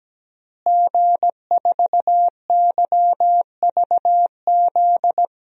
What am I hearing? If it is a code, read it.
Morse code, heard as a keyed tone, about 17 wpm, 700 Hz. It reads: G4YVZ
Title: DXCC-100 Call Signs - 5 Characters - Repeated 2x - 17wpm